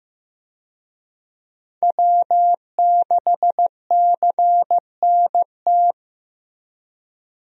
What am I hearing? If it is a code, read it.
Morse code, heard as a keyed tone, about 15 wpm, 700 Hz. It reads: W6CNT
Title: USA Call Signs - 5 Characters - Repeated 2x - 15wpm